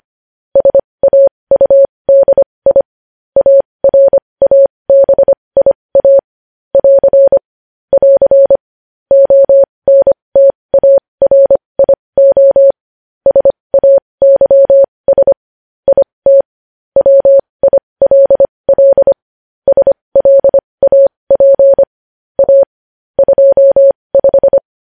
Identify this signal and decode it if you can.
Morse code, heard as a keyed tone, about 25 wpm, spaced as for 20 wpm, 550 Hz. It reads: SAUDI ARABIA <AR> <AR> ONTARIO SAYS IT WILL SLAP A 25
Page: News Headlines in Morse Code at 20 WPM